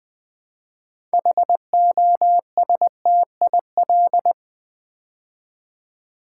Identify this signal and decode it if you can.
Morse code, heard as a keyed tone, about 20 wpm, 700 Hz. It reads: HOSTIL
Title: Build A Word - 3 Letters At A Time - 9-Letter Words - 20wpm